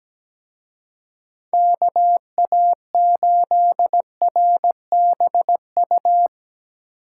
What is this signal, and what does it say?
Morse code, heard as a keyed tone, about 19 wpm, 700 Hz. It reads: KA8RBU
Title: Speed Racing - USA 2x3 Call Signs 17wpm